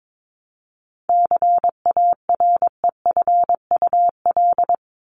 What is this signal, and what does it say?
Morse code, heard as a keyed tone, about 22 wpm, 700 Hz. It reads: CAREFUL
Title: Top 1501-1750 Words 22wpm